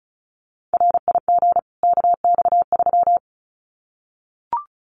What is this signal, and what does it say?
Morse code, heard as a keyed tone, about 34 wpm, 700 Hz. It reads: RIG KX3
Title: Common QSO Phrases 35wpm